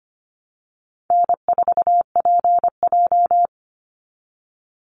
Morse code, heard as a keyed tone, about 25 wpm, 700 Hz. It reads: N4PJ
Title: Call Signs - 25wpm